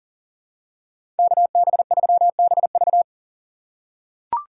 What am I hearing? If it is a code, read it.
Morse code, heard as a keyed tone, about 39 wpm, 700 Hz. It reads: KB3BV
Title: USA Call Signs - 6x Character Spacing - 40wpm